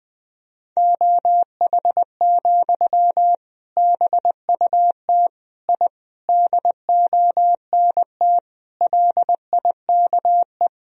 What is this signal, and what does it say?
Morse code, heard as a keyed tone, about 20 wpm, 700 Hz. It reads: OH, BUT I DONT LIKE
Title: Sentences from Top 1000 Words - Review Part 3 - 20wpm